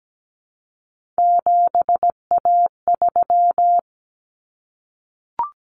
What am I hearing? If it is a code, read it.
Morse code, heard as a keyed tone, about 17 wpm, 700 Hz. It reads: 7A3
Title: Common Cut Numbers - 3-Digit 17wpm